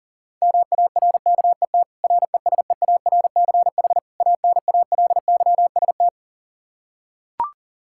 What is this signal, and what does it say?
Morse code, heard as a keyed tone, about 40 wpm, 700 Hz. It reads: MARKET RESEARCH ANALYST
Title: Practice QSO - Exchange 3 - More Info - 40wpm